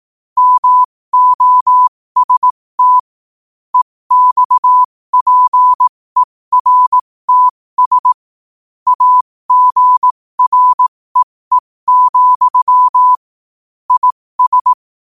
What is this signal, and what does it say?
Morse code, heard as a keyed tone, about 18 wpm, spaced as for 15 wpm, 1000 Hz. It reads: MOST EXPERTS AGREE, IS